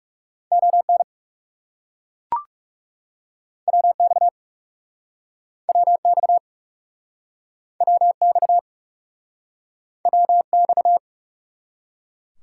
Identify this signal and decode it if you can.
Morse code, heard as a keyed tone, about 40 wpm, 700 Hz. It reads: ON WX WX WX WX